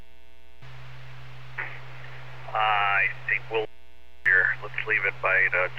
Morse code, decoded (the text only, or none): none